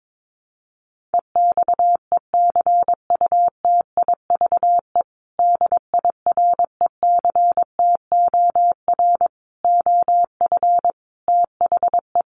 Morse code, heard as a keyed tone, about 22 wpm, 700 Hz. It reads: EXECUTIVE DIRECTOR OF THE
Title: Sets of 5 Words Encore2 22wpm